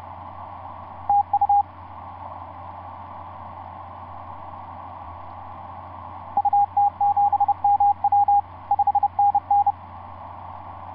Morse code, heard as a keyed tone, about 30 wpm, 810 Hz. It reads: TU UT7MW 5NN
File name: s79gm on4on etc 15m cw 30mar2010 0653